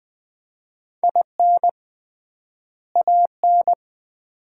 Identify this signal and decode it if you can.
Morse code, heard as a keyed tone, about 20 wpm, 700 Hz. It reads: IN AN
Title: 3x Word-Spacing - Sets of 2 Words - 20wpm